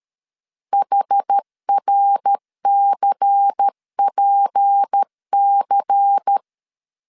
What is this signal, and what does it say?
Morse code, heard as a keyed tone, about 13 wpm, 800 Hz. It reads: HRCPC
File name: CW / DET / 1 6 1500C S B 12wpm 3431